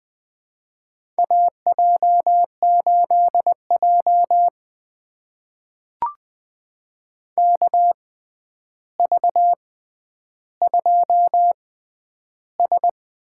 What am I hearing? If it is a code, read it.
Morse code, heard as a keyed tone, about 21 wpm, spaced as for 10 wpm, 700 Hz. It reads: AJ8J KV2S
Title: USA Call Signs - 4 Characters - 6x Character Spacing - 20wpm